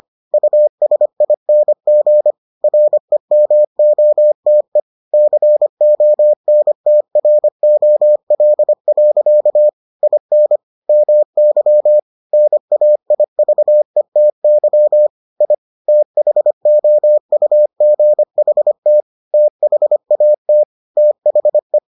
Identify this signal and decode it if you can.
Morse code, heard as a keyed tone, about 25 wpm, 600 Hz. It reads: USING REMOTE CONTROL. IN MY NAIVETY I THOUGHT THAT THE